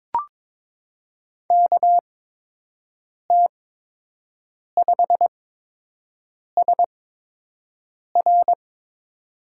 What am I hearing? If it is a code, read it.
Morse code, heard as a keyed tone, about 22 wpm, spaced as for 6 wpm, 700 Hz. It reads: KT5SR
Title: USA Call Signs - 5 Characters - 8x Character Spacing - 22wpm